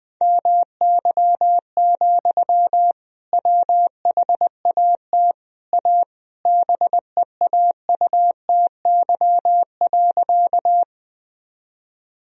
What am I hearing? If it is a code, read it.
Morse code, heard as a keyed tone, about 20 wpm, 700 Hz. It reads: MY, WHAT A BEAUTY.
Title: Sentences from Top 1000 Words - Review Part 2 - 20wpm